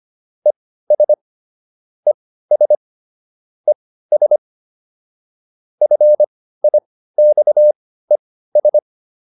Morse code, heard as a keyed tone, about 25 wpm, spaced as for 15 wpm, 600 Hz. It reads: ES ES ES FIXES